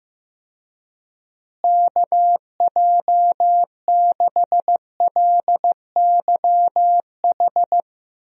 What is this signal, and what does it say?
Morse code, heard as a keyed tone, about 15 wpm, 700 Hz. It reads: KJ6LYH
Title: USA Call Signs - 6 Characters - Repeated 2x - 15wpm